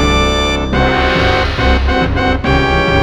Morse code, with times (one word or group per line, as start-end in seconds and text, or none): none